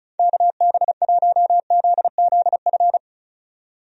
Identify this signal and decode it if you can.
Morse code, heard as a keyed tone, about 36 wpm, 700 Hz. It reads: KB1ZZF
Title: USA Call Signs - 6 Characters - 35wpm